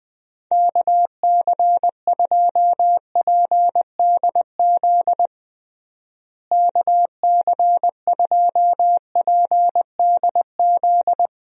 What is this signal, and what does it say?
Morse code, heard as a keyed tone, about 20 wpm, 700 Hz. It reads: KC2PDZ KC2PDZ
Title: USA Call Signs - 6 Characters - Repeated 3x - 20wpm